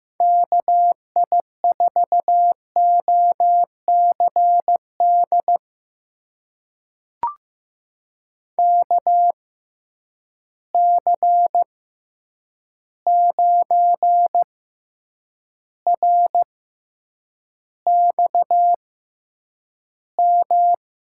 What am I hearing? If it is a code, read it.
Morse code, heard as a keyed tone, about 15 wpm, spaced as for 8 wpm, 700 Hz. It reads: KI4OCD KC9RXM
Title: USA Call Signs - 6 Characters - 6x Character Spacing - 15wpm